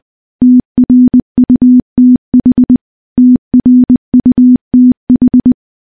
Morse code, heard as a keyed tone, about 20 wpm, 250 Hz. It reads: TRUTH TRUTH
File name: morse-training-20WPM